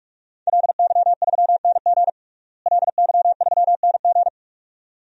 Morse code, heard as a keyed tone, about 44 wpm, 700 Hz. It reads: LY3NG LY3NG
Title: DXCC-100 Call Signs - 5 Characters - Repeated 2x - 45wpm